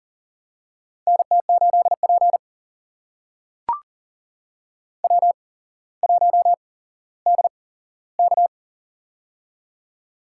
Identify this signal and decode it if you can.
Morse code, heard as a keyed tone, about 39 wpm, spaced as for 17 wpm, 700 Hz. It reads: NT8P W1DK
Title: USA Call Signs - 8x Character Spacing - 40wpm